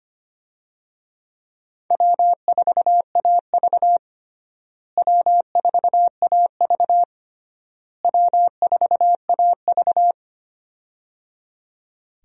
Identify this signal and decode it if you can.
Morse code, heard as a keyed tone, about 25 wpm, 700 Hz. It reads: W4AV W4AV W4AV
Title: USA Call Signs - 4 Characters - Repeated 3x - 25wpm